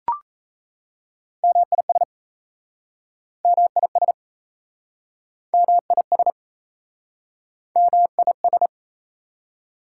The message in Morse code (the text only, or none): MIS MIS MIS MIS